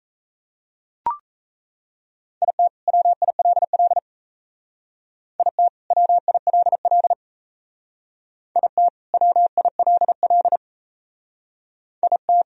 Morse code, heard as a keyed tone, about 35 wpm, 700 Hz. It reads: IT WILL IT WILL IT WILL IT